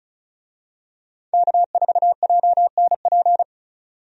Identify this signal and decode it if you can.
Morse code, heard as a keyed tone, about 36 wpm, 700 Hz. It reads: K4JNP
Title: USA Call Signs - 5 Characters - 35wpm